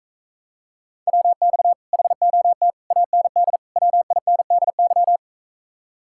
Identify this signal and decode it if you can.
Morse code, heard as a keyed tone, about 44 wpm, 700 Hz. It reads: WX HOT AND WINDY